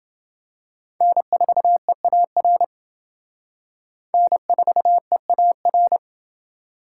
Morse code, heard as a keyed tone, about 29 wpm, 700 Hz. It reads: N4EAR N4EAR